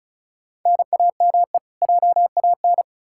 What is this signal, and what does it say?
Morse code, heard as a keyed tone, about 36 wpm, 700 Hz. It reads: NAME JAN
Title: Practice QSO - Exchange 1 - Beginning - 35wpm